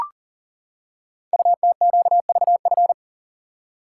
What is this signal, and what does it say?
Morse code, heard as a keyed tone, about 38 wpm, 690 Hz. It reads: UTQVF
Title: Code Groups - 5 Letters - 40wpm